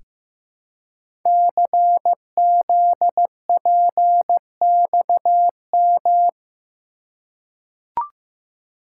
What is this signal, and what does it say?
Morse code, heard as a keyed tone, about 15 wpm, 700 Hz. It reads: CZPXM
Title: Code Groups - 5 Letters - 15wpm